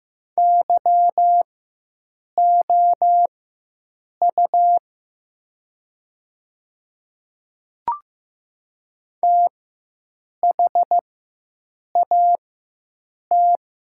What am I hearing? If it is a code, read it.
Morse code, heard as a keyed tone, about 15 wpm, spaced as for 7 wpm, 700 Hz. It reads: YOU THAT
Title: 4x Character Spacing - Top 300 Words - 15wpm